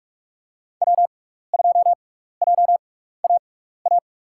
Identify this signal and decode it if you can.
Morse code, heard as a keyed tone, about 44 wpm, spaced as for 15 wpm, 700 Hz. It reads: W2JAA
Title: USA Call Signs - 5 Characters - 6x Character Spacing - 45wpm